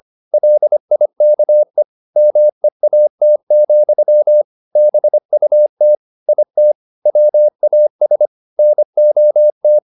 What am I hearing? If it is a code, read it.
Morse code, heard as a keyed tone, about 25 wpm, 600 Hz. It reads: LIKE MEAT, BUT IT WAS NOT